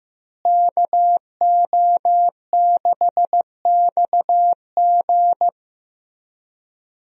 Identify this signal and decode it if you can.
Morse code, heard as a keyed tone, about 15 wpm, 700 Hz. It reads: KO6XG
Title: USA Call Signs - 5 Characters - Repeated 3x - 15wpm